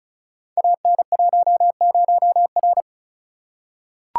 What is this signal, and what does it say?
Morse code, heard as a keyed tone, about 36 wpm, 700 Hz. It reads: AN10R
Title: Code Groups - 5 Letter:Numbers - 35wpm